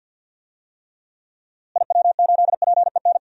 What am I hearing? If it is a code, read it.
Morse code, heard as a keyed tone, about 52 wpm, 700 Hz. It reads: IW8PEN
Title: DXCC-100 Call Signs - 6 Characters - Repeated 2x - 50wpm